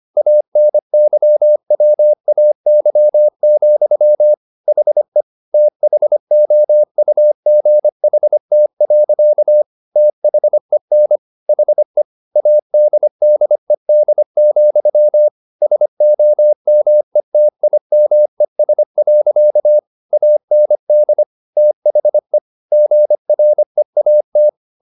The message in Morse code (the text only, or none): ANYWAY, HE THOUGHT. THEN HE ADDED, SOMETIMES. AND THE GREAT